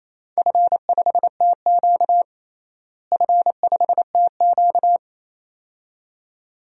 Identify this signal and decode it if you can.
Morse code, heard as a keyed tone, about 27 wpm, 700 Hz. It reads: F5TQ F5TQ
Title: DXCC-100 Call Signs - 4 Characters - Repeated 2x - 28wpm